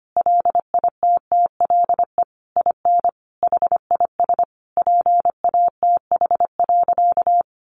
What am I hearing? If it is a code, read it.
Morse code, heard as a keyed tone, about 25 wpm, 700 Hz. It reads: LITTLE IN HIS PATH.
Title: Sentences from Top 1400 Words 25wpm